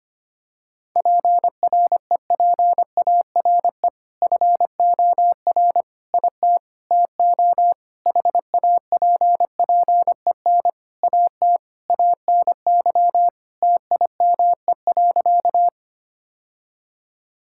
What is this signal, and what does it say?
Morse code, heard as a keyed tone, about 25 wpm, 700 Hz. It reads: PREPARE FOR IT TO HAPPEN AT ANY TIME.
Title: Sentences from Top 1000 Words - Review Part 6 - 25wpm